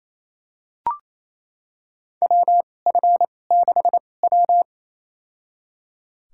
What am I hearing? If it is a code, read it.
Morse code, heard as a keyed tone, about 28 wpm, spaced as for 20 wpm, 700 Hz. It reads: WF6W
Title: USA Call Signs - 4 Characters - 2x Character Spacing - 28wpm